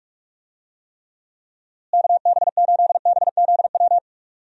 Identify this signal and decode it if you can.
Morse code, heard as a keyed tone, about 44 wpm, 700 Hz. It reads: KB8BZW
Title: USA Call Signs - 6 Characters - Repeated 3x - 45wpm